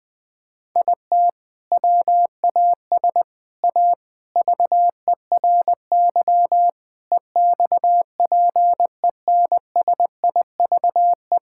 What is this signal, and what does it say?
Morse code, heard as a keyed tone, about 21 wpm, 700 Hz. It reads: IT WAS A VERY EXPENSIVE
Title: Sentences from Top 1000 Words - Review Part 6 - 20wpm